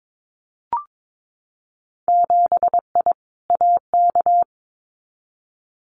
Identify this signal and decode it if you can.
Morse code, heard as a keyed tone, about 22 wpm, 700 Hz. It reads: 7I AK
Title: Winter Field Day Exchanges Encore 22wpm